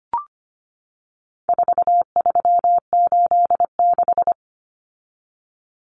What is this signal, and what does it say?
Morse code, heard as a keyed tone, about 25 wpm, 700 Hz. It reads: 4386
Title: Common Cut Numbers - 4-Digit 25wpm